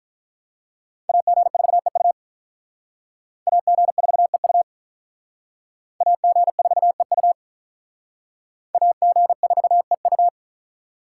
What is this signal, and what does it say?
Morse code, heard as a keyed tone, about 43 wpm, 700 Hz. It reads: AG4EU AG4EU AG4EU AG4EU
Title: Speed Racing - USA 2x2 Call Signs 35wpm